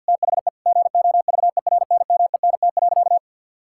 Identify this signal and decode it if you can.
Morse code, heard as a keyed tone, about 52 wpm, 700 Hz. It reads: THE GOVERNMENT.
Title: Sentences from Top 1600 Words 50wpm